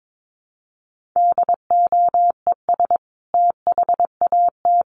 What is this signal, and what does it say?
Morse code, heard as a keyed tone, about 22 wpm, 700 Hz. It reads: DOES THAT
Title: Sentences from Top 1300 Words 22wpm